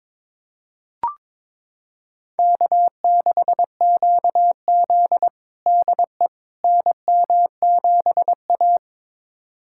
Morse code, heard as a keyed tone, about 22 wpm, 700 Hz. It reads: K6QZ DE NM7A
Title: USA Callsign DE Callsign - 4 Characters - 22wpm